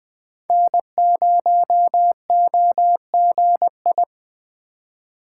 Morse code, heard as a keyed tone, about 20 wpm, 700 Hz. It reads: N0OGI
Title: USA Call Signs - 5 Characters - 3x Character Spacing - 20wpm